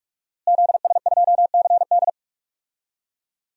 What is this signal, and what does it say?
Morse code, heard as a keyed tone, about 46 wpm, 700 Hz. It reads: ZS2CD
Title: DXCC-100 Call Signs - 5 Characters - Repeated 2x - 45wpm